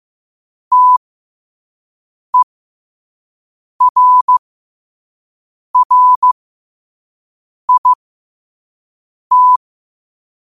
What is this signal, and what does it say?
Morse code, heard as a keyed tone, about 15 wpm, spaced as for 5 wpm, 1000 Hz. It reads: TERRIT
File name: qotdfiles / 5wpm